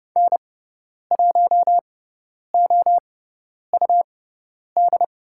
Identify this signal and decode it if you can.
Morse code, heard as a keyed tone, about 30 wpm, spaced as for 10 wpm, 700 Hz. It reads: N1OUD
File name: Lesson 17g - Letter u - Call Signs 30wpm